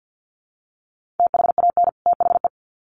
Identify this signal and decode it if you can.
Morse code, heard as a keyed tone, about 50 wpm, 700 Hz. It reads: THAN THE
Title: Sets of 2 Words 50wpm